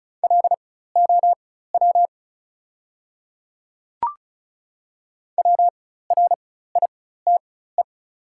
Morse code, heard as a keyed tone, about 34 wpm, spaced as for 16 wpm, 690 Hz. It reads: LOW WRITE
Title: 4x Character Spacing - Top 200 Words - 35wpm